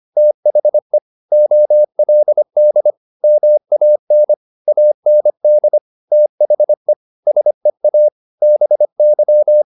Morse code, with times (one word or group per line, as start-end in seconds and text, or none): 0.2-1.0 THE
1.3-2.9 OLD
3.2-4.3 MAN
4.7-5.8 AND
6.1-6.9 THE
7.3-8.1 SEA
8.4-9.6 BY